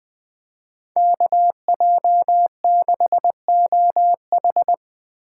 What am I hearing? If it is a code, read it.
Morse code, heard as a keyed tone, about 20 wpm, 700 Hz. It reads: KJ6OH